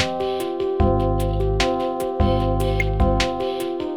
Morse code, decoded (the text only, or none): none